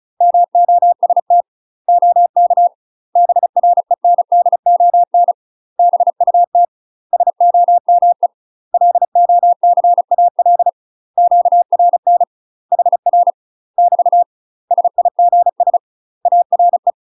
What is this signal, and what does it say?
Morse code, heard as a keyed tone, about 34 wpm, 700 Hz. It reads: MOST OK BRENDON BUT SOME LOCAL QRN HR <BT> SIGS ARE
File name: UK MASTER QSOs TRAINING / 35WPM / UK MASTER QSOs 0025